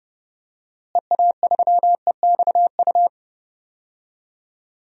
Morse code, heard as a keyed tone, about 30 wpm, 700 Hz. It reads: EA3EXU
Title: DXCC-100 Call Signs - 6 Characters - Repeated 2x - 30wpm